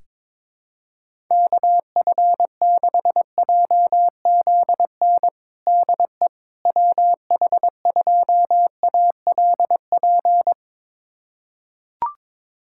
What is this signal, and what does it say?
Morse code, heard as a keyed tone, about 22 wpm, 700 Hz. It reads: KF6JZN DE WH2ALP
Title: USA Callsign DE Callsign - 6 Characters - 22wpm